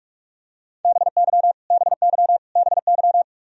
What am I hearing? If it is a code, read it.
Morse code, heard as a keyed tone, about 44 wpm, 700 Hz. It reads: BY BY BY
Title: Top 100 Words Repeated 45wpm